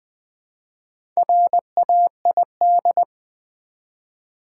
Morse code, heard as a keyed tone, about 20 wpm, 700 Hz. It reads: RAID
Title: Build A Word - 1 Letter At A Time - 4-Letter Words - 20wpm